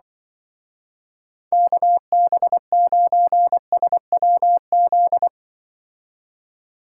Morse code, heard as a keyed tone, about 23 wpm, 700 Hz. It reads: KB9SWZ
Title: Speed Racing - USA Call Signs 20wpm